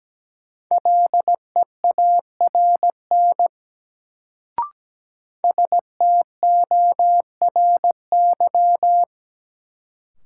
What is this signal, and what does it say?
Morse code, heard as a keyed tone, about 17 wpm, 700 Hz. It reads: LEARN STORY
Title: Sentences from Top 300 Words - New Word Drill - 17wpm